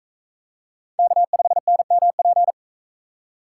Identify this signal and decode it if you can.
Morse code, heard as a keyed tone, about 41 wpm, 700 Hz. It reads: K5NMP